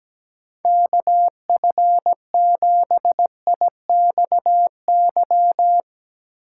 Morse code, heard as a keyed tone, about 17 wpm, 700 Hz. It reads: KF7IXY
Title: Speed Racing - USA Call Signs 17wpm